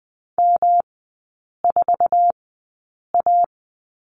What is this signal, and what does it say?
Morse code, heard as a keyed tone, about 20 wpm, spaced as for 8 wpm, 700 Hz. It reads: M4A
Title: 107 Lesson 18e - Letter m - Three Characters 20wpm